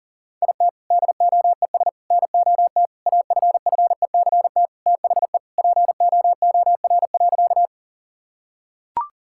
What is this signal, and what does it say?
Morse code, heard as a keyed tone, about 37 wpm, 700 Hz. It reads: IT DOES NOT AFFECT THE POOR.
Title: Sentences from Top 1000 Words - Review Part 6 - 40wpm